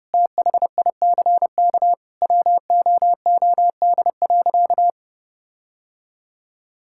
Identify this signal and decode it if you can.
Morse code, heard as a keyed tone, about 31 wpm, 700 Hz. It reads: THICK WOOD.
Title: Sentences from Top 1600 Words 30wpm